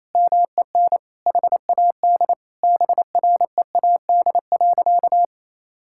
Morse code, heard as a keyed tone, about 27 wpm, 700 Hz. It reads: MEN HAD BREAD.
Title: Sentences from Top 1600 Words 28wpm